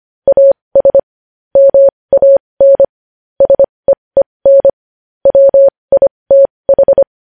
Morse code, heard as a keyed tone, about 25 wpm, spaced as for 20 wpm, 550 Hz. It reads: AS MAN SEEN WITH
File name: podcast / News headlines-1x-25x20